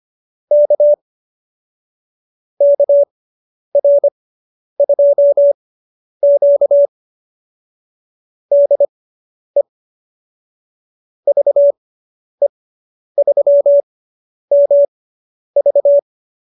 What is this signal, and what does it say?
Morse code, heard as a keyed tone, about 25 wpm, spaced as for 10 wpm, 590 Hz. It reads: K KR2Q DE VE3MV
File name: QSO220 10